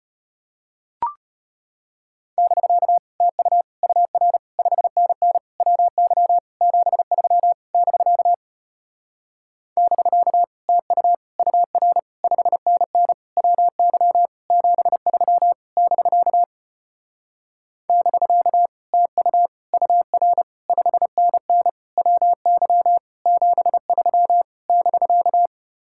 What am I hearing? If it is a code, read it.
Morse code, heard as a keyed tone, about 33 wpm, 700 Hz. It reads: <BK> TU UR 5NN WY 73 <BK> <BK> TU UR 5NN WY 73 <BK> <BK> TU UR 5NN WY 73 <BK>